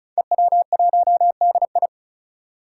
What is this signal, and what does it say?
Morse code, heard as a keyed tone, about 36 wpm, 700 Hz. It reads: EW1DI
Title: Five Challenging Letter-Number Combinations 35wpm